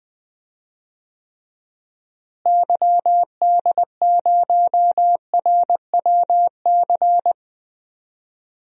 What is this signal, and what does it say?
Morse code, heard as a keyed tone, about 20 wpm, 700 Hz. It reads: YD0RWC
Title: DXCC-100 Call Signs - 6 Characters - Repeated 2x - 20wpm